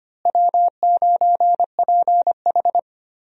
Morse code, heard as a keyed tone, about 25 wpm, 700 Hz. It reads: W9PH
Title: USA Call Signs - 4 Characters - Repeated 3x - 25wpm